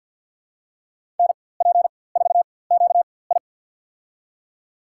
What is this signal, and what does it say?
Morse code, heard as a keyed tone, about 52 wpm, spaced as for 23 wpm, 700 Hz. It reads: NP4QI
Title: USA Call Signs - 5 Characters - 4x Character Spacing - 50wpm